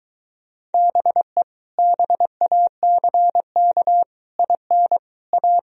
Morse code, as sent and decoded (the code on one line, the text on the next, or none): -... / -... .- -.-. -.- / .. -. / .-
BE BACK IN A